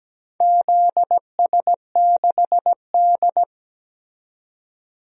Morse code, heard as a keyed tone, about 17 wpm, 700 Hz. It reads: ZS6D
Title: DXCC-100 Call Signs - 4 Characters - 17wpm